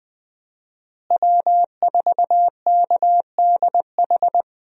Morse code, heard as a keyed tone, about 21 wpm, 700 Hz. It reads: W4KDH
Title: USA Call Signs - 5 Characters - Repeated 2x - 20wpm